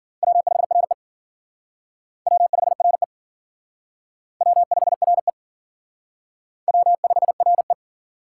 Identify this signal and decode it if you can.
Morse code, heard as a keyed tone, about 50 wpm, 700 Hz. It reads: W5RE W5RE W5RE W5RE